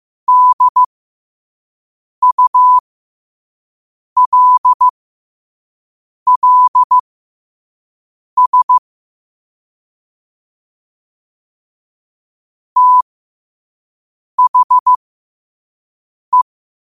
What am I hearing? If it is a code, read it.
Morse code, heard as a keyed tone, about 15 wpm, spaced as for 5 wpm, 1000 Hz. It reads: DULLS THE
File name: 5wpm